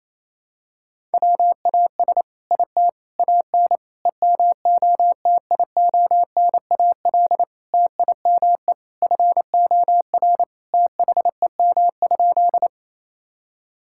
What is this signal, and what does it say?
Morse code, heard as a keyed tone, about 28 wpm, 700 Hz. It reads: WAS IT AN EMOTIONAL TIME FOR THEM?
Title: Sentences from Top 1000 Words - Review Part 5 - 28wpm